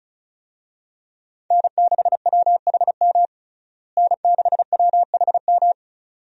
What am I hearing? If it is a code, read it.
Morse code, heard as a keyed tone, about 36 wpm, 700 Hz. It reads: N6WHM N6WHM
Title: USA Call Signs - 5 Characters - Repeated 2x - 35wpm